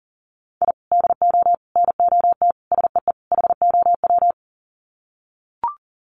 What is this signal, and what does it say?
Morse code, heard as a keyed tone, about 40 wpm, 700 Hz. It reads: I DO NOT SEE HOW
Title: Sets of 5 Words 40wpm